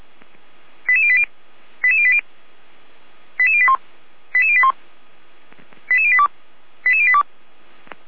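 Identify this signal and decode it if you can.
Five-tone selective call (ZVEI), heard as quick runs of five tones, tones of about 70 ms. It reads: 88080 88080 88081 88081 88082 88082
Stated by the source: ZVEI five-tone selective call 88084 received on an SDR, Feuerwehr Orlamünde.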